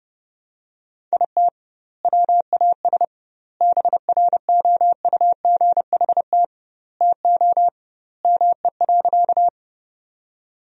Morse code, heard as a keyed tone, about 32 wpm, 700 Hz. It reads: IT WAS BROUGHT TO ME.
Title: 2x Word-Spacing - Sentences from Top 500 Words - 30wpm